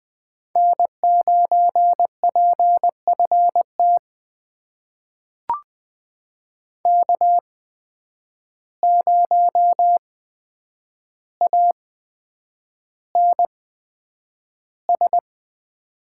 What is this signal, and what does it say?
Morse code, heard as a keyed tone, about 20 wpm, spaced as for 9 wpm, 700 Hz. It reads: N9PFT K0ANS
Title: USA Call Signs - 5 Characters - 8x Character Spacing - 20wpm